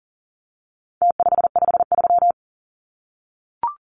Morse code, heard as a keyed tone, about 40 wpm, 700 Hz. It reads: T553
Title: Numbers - Contest Style - 4-Digit 40wpm